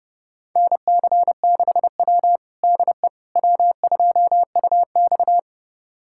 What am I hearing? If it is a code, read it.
Morse code, heard as a keyed tone, about 30 wpm, 700 Hz. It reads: NC6W DE W2UX
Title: USA Callsign DE Callsign - 4 Characters - 30wpm